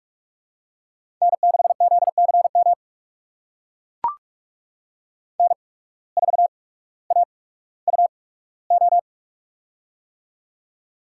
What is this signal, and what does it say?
Morse code, heard as a keyed tone, about 44 wpm, spaced as for 19 wpm, 700 Hz. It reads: N6ZCM N4AUO
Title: USA Call Signs - 5 Characters - 8x Character Spacing - 45wpm